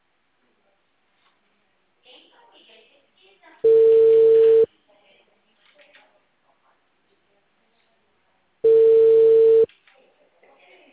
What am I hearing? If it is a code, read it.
Ringback tone (China, 450 Hz), with speech.